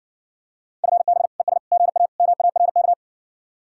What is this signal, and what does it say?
Morse code, heard as a keyed tone, about 61 wpm, 700 Hz. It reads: FB ES GA MARK